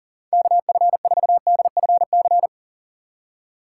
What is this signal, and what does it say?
Morse code, heard as a keyed tone, about 40 wpm, 700 Hz. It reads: KF4DFC
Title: USA Call Signs - 6x Character Spacing - 40wpm